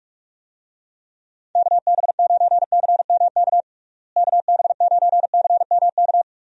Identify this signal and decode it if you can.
Morse code, heard as a keyed tone, about 42 wpm, 700 Hz. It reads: KB9CMK KB9CMK
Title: USA Call Signs - 6 Characters - Repeated 2x - 45wpm